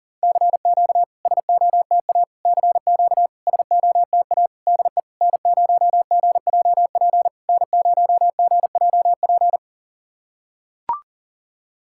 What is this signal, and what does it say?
Morse code, heard as a keyed tone, about 40 wpm, 690 Hz. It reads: CQ SOTA CQ SOTA DE N0GJP N0GJP
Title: SOTA QSO Practice 40wpm